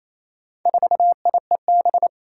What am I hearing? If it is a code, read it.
Morse code, heard as a keyed tone, about 29 wpm, 700 Hz. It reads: 4IEB
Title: Four Challenging Letter-Number Combinations 28wpm